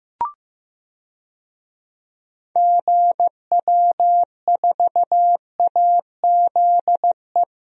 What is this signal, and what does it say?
Morse code, heard as a keyed tone, about 15 wpm, 700 Hz. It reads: GW4AZE
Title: DXCC-100 Call Signs - 6 Characters - 15wpm